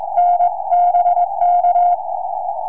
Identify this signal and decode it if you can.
Morse code, heard as a keyed tone, about 20 wpm, 720 Hz. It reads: NBK
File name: tropo / g0nce 2m